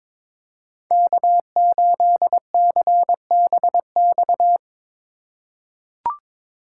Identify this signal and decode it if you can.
Morse code, heard as a keyed tone, about 22 wpm, 700 Hz. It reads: K8CBX
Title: USA Call Signs - 5 Characters - 4x Character Spacing - 22wpm